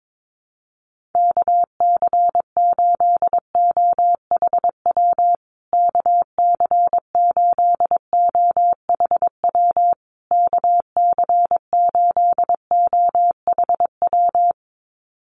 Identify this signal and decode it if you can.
Morse code, heard as a keyed tone, about 22 wpm, 700 Hz. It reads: KC8OHW KC8OHW KC8OHW
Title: USA Call Signs Repeated 3x 22wpm